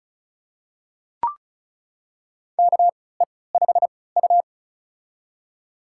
Morse code, heard as a keyed tone, about 34 wpm, spaced as for 20 wpm, 700 Hz. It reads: KE5U
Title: USA Call Signs - 4 Characters - 3x Character Spacing - 35wpm